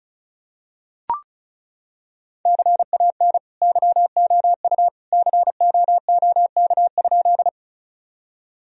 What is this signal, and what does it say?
Morse code, heard as a keyed tone, about 34 wpm, 700 Hz. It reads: CAN YOU COOK?
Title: Sentences from Top 1000 Words - Review Part 4 - 35wpm